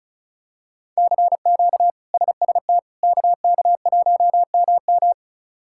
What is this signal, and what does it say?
Morse code, heard as a keyed tone, about 34 wpm, 700 Hz. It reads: CQ SST KK1MM
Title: K1USN SST QSO Practice - 35wpm